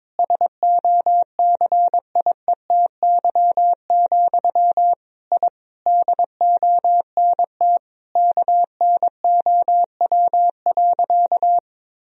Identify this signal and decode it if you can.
Morse code, heard as a keyed tone, about 22 wpm, 700 Hz. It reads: SOCIETY, I DONT KNOW.
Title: Sentences from Top 1000 Words - Review Part 4 - 22wpm